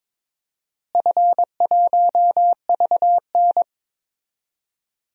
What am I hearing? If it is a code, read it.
Morse code, heard as a keyed tone, about 22 wpm, 700 Hz. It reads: F1VN
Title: DXCC-100 Call Signs - 4 Characters - Repeated 3x - 22wpm